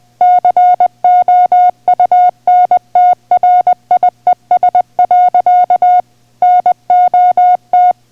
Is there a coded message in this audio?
Yes: Morse code from 0.2 to 7.9 s, heard as a keyed tone, about 20 wpm, 720 Hz. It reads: COUNTRIES. NOT